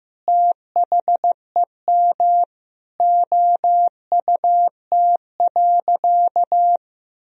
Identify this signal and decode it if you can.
Morse code, heard as a keyed tone, about 15 wpm, 700 Hz. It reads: THEM OUT.
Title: Sentences from Top 500 Words and Abbreviations 15wpm